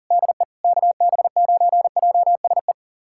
Morse code, heard as a keyed tone, about 37 wpm, 700 Hz. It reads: DE KB9JSE
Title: USA Callsign DE Callsign 40wpm